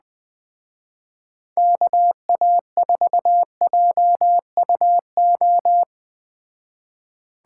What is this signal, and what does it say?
Morse code, heard as a keyed tone, about 20 wpm, 700 Hz. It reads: KA4JUO
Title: Speed Racing - USA 2x3 Call Signs 17wpm